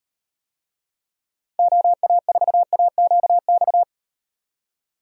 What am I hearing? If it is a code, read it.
Morse code, heard as a keyed tone, about 38 wpm, 700 Hz. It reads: OA4AQX